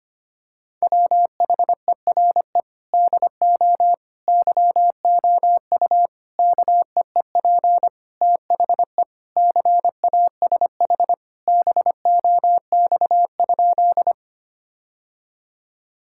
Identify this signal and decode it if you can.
Morse code, heard as a keyed tone, about 25 wpm, 700 Hz. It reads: WHERE DO YOU KEEP THE CASH BOX?
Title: Sentences from Top 1000 Words - Review Part 5 - 25wpm